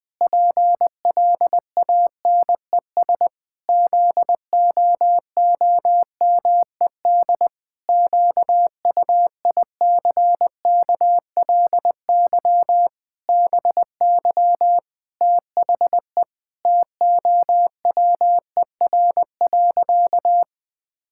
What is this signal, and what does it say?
Morse code, heard as a keyed tone, about 21 wpm, 700 Hz. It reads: PLANES ZOOMED QUICKLY BY THE TOWER.
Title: Pangram Challenge v2 - 20wpm